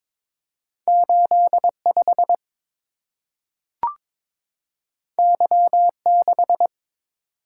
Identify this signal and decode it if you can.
Morse code, heard as a keyed tone, about 22 wpm, 690 Hz. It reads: 85 Y6
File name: practice / 22wpm / Two Letter-Number Combinations - 22wpm